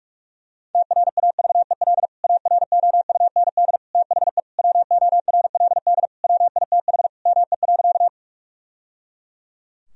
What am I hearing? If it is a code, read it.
Morse code, heard as a keyed tone, about 46 wpm, 700 Hz. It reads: TRAVEL AROUND THE WORLD WITH ME.
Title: Sentences from Top 1000 Words - Review Part 3 - 45wpm